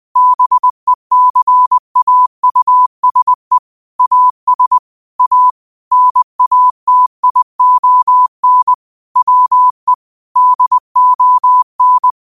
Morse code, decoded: BECAUSE AS A NATION WE DON

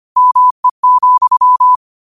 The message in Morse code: ME,

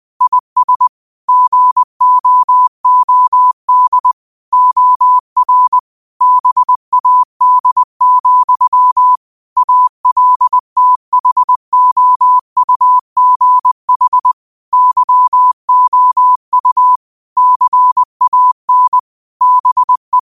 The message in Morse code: IS GOOD OR BAD, ALTHOUGH YOU CAN BE

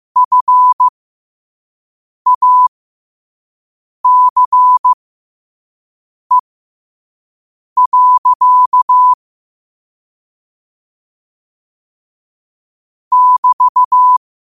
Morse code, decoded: FACE. <BT>

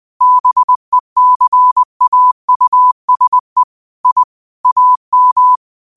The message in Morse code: BECAUSE I AM

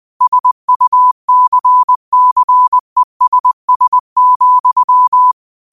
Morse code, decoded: SUCCESS,

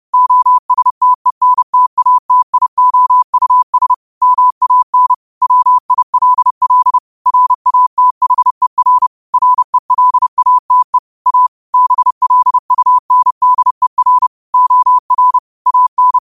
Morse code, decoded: OSTENTATIOUS MAN WILL RATHER RELATE A BLUNDER OR AN

